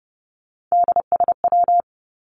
Morse code, heard as a keyed tone, about 30 wpm, 700 Hz. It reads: DSW
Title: Abbreviations Repeated 3x - 30wpm